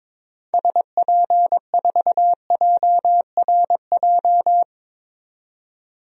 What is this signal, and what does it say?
Morse code, heard as a keyed tone, about 22 wpm, 700 Hz. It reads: SP4JRJ
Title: DXCC-100 Call Signs - 6 Characters - 22wpm